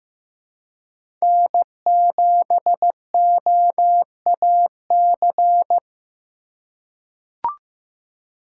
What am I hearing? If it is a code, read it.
Morse code, heard as a keyed tone, about 15 wpm, 700 Hz. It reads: N7OAC
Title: USA Call Signs - 5 Characters - 4x Character Spacing - 15wpm